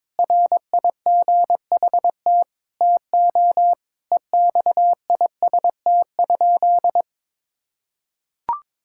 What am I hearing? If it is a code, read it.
Morse code, heard as a keyed tone, about 22 wpm, 700 Hz. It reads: RIGHT TO EXIST?
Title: Sentences from Top 1000 Words - Review Part 5 - 22wpm